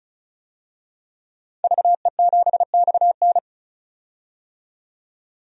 Morse code, heard as a keyed tone, about 34 wpm, 700 Hz. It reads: VE7XN
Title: DXCC-100 Call Signs - 5 Characters - 35wpm